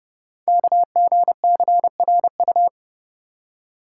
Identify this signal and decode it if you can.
Morse code, heard as a keyed tone, about 28 wpm, 700 Hz. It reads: KGCRU
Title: Code Groups - 5 Letters - 30wpm